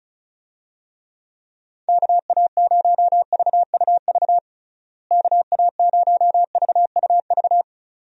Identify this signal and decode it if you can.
Morse code, heard as a keyed tone, about 34 wpm, 700 Hz. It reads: KA0VUV KA0VUV
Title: USA Call Signs - 6 Characters - Repeated 2x - 35wpm